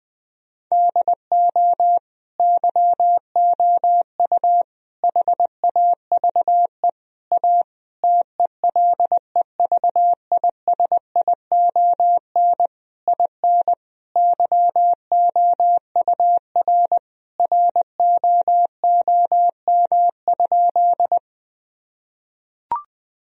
Morse code, heard as a keyed tone, about 20 wpm, 700 Hz. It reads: DO YOU HAVE A TELEVISION IN YOUR ROOM?
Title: Sentences from Top 1000 Words - Review Part 4 - 20wpm